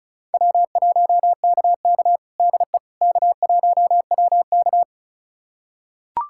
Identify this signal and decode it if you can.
Morse code, heard as a keyed tone, about 34 wpm, 700 Hz. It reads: W1KK DE K1WK
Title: USA Callsign DE Callsign - 4 Characters - 35wpm